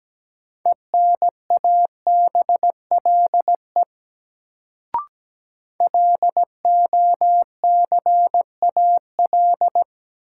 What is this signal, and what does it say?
Morse code, heard as a keyed tone, about 17 wpm, 700 Hz. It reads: ENABLE LOCAL
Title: Sentences from Top 1500 Words - New Word Drill Alt - 17wpm